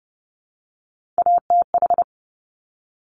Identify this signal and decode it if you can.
Morse code, heard as a keyed tone, about 30 wpm, 690 Hz. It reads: ATH
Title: Top Three Letter Combinations in Words 30wpm